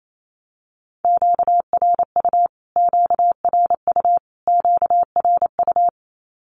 Morse code, heard as a keyed tone, about 28 wpm, 700 Hz. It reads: QRU QRU QRU
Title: Q-Codes Repeated 3x 28wpm